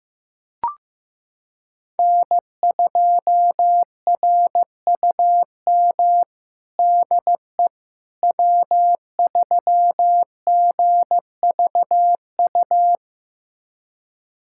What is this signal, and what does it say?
Morse code, heard as a keyed tone, about 15 wpm, 700 Hz. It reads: N2RUM DE W3GVU
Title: USA Callsign DE Callsign - 5 Characters - 15wpm